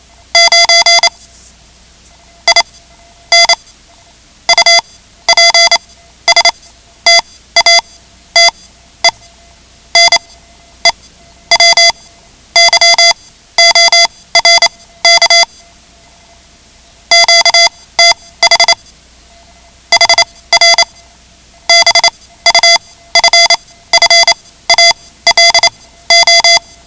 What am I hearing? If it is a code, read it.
Morse code, heard as a keyed tone, about 28 wpm, spaced as for 14 wpm, 2050 Hz. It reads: 9 IN UPSTATE NEW YORK QTH HR BUFFALO
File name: ham radio / CodeQuick / Code Quick 10